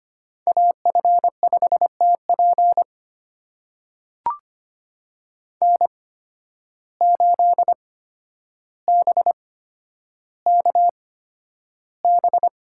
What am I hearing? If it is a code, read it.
Morse code, heard as a keyed tone, about 25 wpm, spaced as for 11 wpm, 700 Hz. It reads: AF5TP N8BKB
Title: USA Call Signs - 5 Characters - 8x Character Spacing - 25wpm